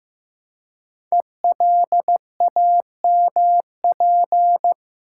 Morse code, heard as a keyed tone, about 15 wpm, 700 Hz. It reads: ELAMP
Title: Code Groups - 5 Letters - 15wpm